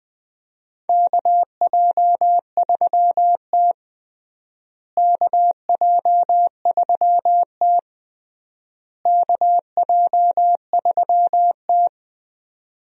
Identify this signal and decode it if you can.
Morse code, heard as a keyed tone, about 20 wpm, 700 Hz. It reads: KJ3T KJ3T KJ3T
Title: USA Call Signs - 4 Characters - Repeated 3x - 20wpm